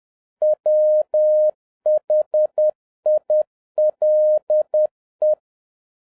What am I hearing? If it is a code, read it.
Morse code, heard as a keyed tone, about 10 wpm, 610 Hz. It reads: WHILE
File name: 2011-07-mp3 / n31 THE 3.11 2 10wpm